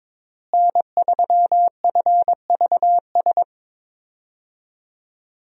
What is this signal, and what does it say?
Morse code, heard as a keyed tone, about 22 wpm, 700 Hz. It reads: N3FVS
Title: USA Call Signs - 5 Characters - Repeated 2x - 22wpm